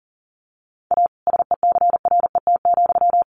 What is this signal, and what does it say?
Morse code, heard as a keyed tone, about 40 wpm, 700 Hz. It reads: A SECRET,
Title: Sentences from Top 1200 Words 40wpm